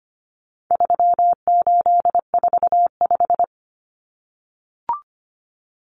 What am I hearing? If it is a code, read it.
Morse code, heard as a keyed tone, about 25 wpm, 700 Hz. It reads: 3845
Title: Common Cut Numbers - 4-Digit 25wpm